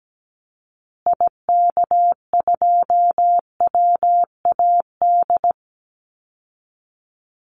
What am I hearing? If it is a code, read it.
Morse code, heard as a keyed tone, about 17 wpm, 700 Hz. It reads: IK2WAD
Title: Call Signs Repeated 2x - 17wpm